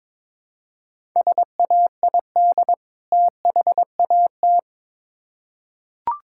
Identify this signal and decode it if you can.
Morse code, heard as a keyed tone, about 22 wpm, 700 Hz. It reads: SAID THAT
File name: Build A Phrase - 2-Word Phrases - 22wpm